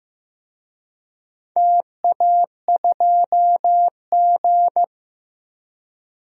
Morse code, heard as a keyed tone, about 15 wpm, 700 Hz. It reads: TA2G
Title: DXCC-100 Call Signs - 4 Characters - 15wpm